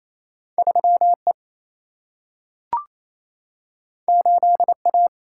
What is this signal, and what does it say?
Morse code, heard as a keyed tone, about 28 wpm, 700 Hz. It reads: 3 E 8 A
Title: Single-Letter Single-Number 28wpm